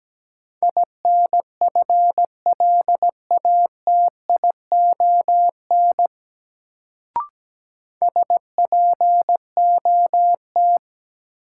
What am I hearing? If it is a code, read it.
Morse code, heard as a keyed tone, about 17 wpm, 700 Hz. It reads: INFLATION SPOT
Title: Sentences from Top 1100 Words - New Word Drill - 17wpm